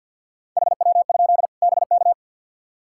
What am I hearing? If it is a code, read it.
Morse code, heard as a keyed tone, about 51 wpm, 700 Hz. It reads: HW? BK
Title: Practice QSO - Exchange 3 - More Info - 50wpm